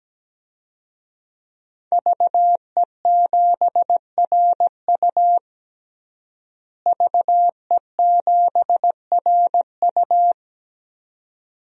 Morse code, heard as a keyed tone, about 17 wpm, 700 Hz. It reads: VE7RU VE7RU
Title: DXCC-100 Call Signs - 5 Characters - Repeated 2x - 17wpm